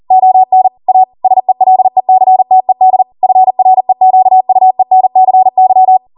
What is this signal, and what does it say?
Morse code, heard as a keyed tone, about 40 wpm, 750 Hz. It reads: ON A SELECTED FREQUENCY
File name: Morse / 40WPM